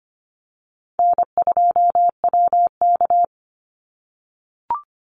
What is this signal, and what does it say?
Morse code, heard as a keyed tone, about 25 wpm, 700 Hz. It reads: N2WK
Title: Call Signs - 25wpm